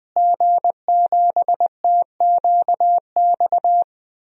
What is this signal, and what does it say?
Morse code, heard as a keyed tone, about 20 wpm, 700 Hz. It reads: G7TQX
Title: DXCC-100 Call Signs - 5 Characters - Repeated 3x - 20wpm